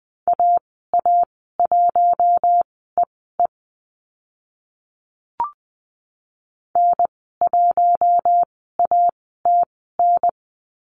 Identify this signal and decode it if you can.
Morse code, heard as a keyed tone, about 20 wpm, spaced as for 14 wpm, 700 Hz. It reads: AA1EE N1ATN
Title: 042 Lesson 8g - Number 1 - Call Signs 20wpm